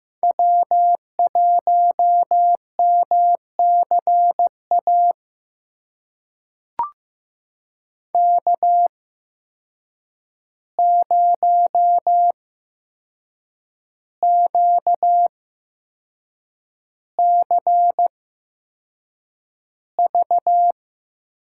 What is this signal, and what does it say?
Morse code, heard as a keyed tone, about 15 wpm, spaced as for 6 wpm, 700 Hz. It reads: W1MCA K0QCV
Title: USA Call Signs - 5 Characters - 8x Character Spacing - 15wpm